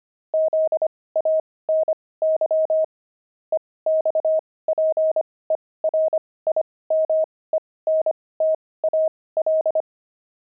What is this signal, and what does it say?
Morse code, heard as a keyed tone, about 25 wpm, spaced as for 18 wpm, 630 Hz. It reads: ZANY EXPERIMENTAL
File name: pangram-fs-18-cs-25-zany